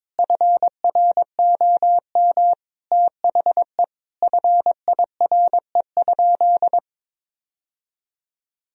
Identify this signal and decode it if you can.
Morse code, heard as a keyed tone, about 22 wpm, 700 Hz. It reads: FROM THE FIRE?
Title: Sentences from Top 1000 Words - Review Part 3 - 22wpm